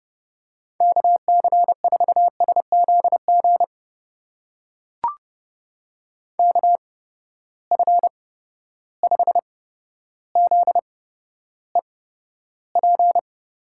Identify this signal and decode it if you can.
Morse code, heard as a keyed tone, about 30 wpm, spaced as for 13 wpm, 700 Hz. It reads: KC4SZG KF5ZEP
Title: USA Call Signs - 6 Characters - 8x Character Spacing - 30wpm